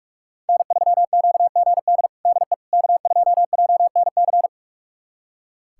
Morse code, heard as a keyed tone, about 44 wpm, 700 Hz. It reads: N3QGD DE K2JNC
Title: USA Callsign DE Callsign - 5 Characters - 45wpm